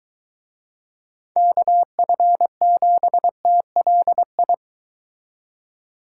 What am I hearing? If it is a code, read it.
Morse code, heard as a keyed tone, about 23 wpm, 700 Hz. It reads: KF7TLI